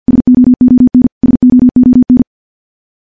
Morse code, heard as a keyed tone, about 50 wpm, 250 Hz. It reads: SOON SOON
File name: morse / morse-training-50WPM